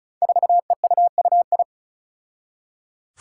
Morse code, heard as a keyed tone, about 36 wpm, 700 Hz. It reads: 4EUUI
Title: Five Challenging Letter-Number Combinations 35wpm